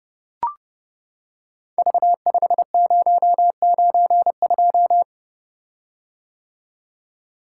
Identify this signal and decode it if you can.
Morse code, heard as a keyed tone, about 30 wpm, 700 Hz. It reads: V5092
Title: Code Groups - 5 Letter:Numbers - 30wpm